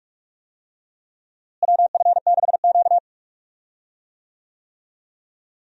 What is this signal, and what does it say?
Morse code, heard as a keyed tone, about 45 wpm, 700 Hz. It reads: WF6Q
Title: USA Call Signs - 4 Characters - Repeated 3x - 45wpm